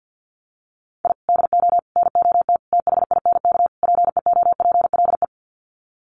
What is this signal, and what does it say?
Morse code, heard as a keyed tone, about 50 wpm, 700 Hz. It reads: I DO NOT THINK PEOPLE
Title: Sets of 5 Words Encore2 50wpm